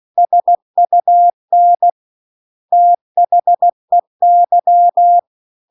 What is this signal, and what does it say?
Morse code, heard as a keyed tone, about 16 wpm, 700 Hz. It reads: SUN THEY